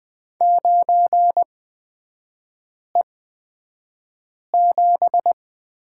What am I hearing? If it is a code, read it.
Morse code, heard as a keyed tone, about 20 wpm, spaced as for 5 wpm, 700 Hz. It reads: 9E7
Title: 243 Lesson 39e - Letter z - Three Characters 20wpm